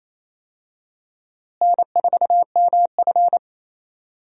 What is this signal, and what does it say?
Morse code, heard as a keyed tone, about 28 wpm, 700 Hz. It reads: N4MF
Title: USA Call Signs - 4 Characters - Repeated 3x - 28wpm